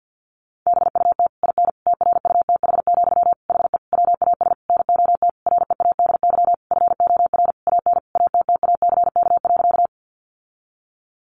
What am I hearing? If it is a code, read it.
Morse code, heard as a keyed tone, about 48 wpm, 700 Hz. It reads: BUT IN TRUTH, HE WAS NOT READY FOR AN ATTACK.
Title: Sentences from Top 1500 Words 50wpm